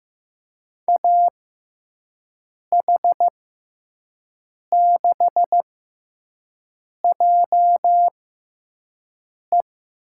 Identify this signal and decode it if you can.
Morse code, heard as a keyed tone, about 15 wpm, spaced as for 5 wpm, 700 Hz. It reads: AH6JE
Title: USA Call Signs - 5 Characters - 6x Character Spacing - 15wpm